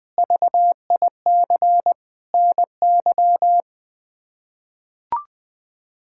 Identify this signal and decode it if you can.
Morse code, heard as a keyed tone, about 20 wpm, 700 Hz. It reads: VIC NY
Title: K1USN SST QSO Practice - 20wpm